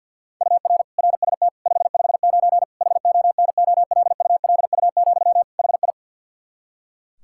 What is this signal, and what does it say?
Morse code, heard as a keyed tone, about 45 wpm, 700 Hz. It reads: UR RST 559 HONOLULU, HI
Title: Practice QSO - Exchange 1 - Beginning - 50wpm